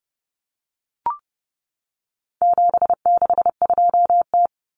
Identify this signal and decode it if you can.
Morse code, heard as a keyed tone, about 30 wpm, 700 Hz. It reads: 762T
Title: Common Cut Numbers - 4-Digit 30wpm